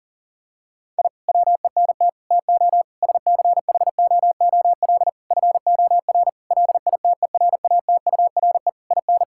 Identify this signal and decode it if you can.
Morse code, heard as a keyed tone, about 40 wpm, 700 Hz. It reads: I WENT TO SCHOOL FOR LITERATURE IN
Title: Sentences from Top 1000 Words - Review Part 6 - 40wpm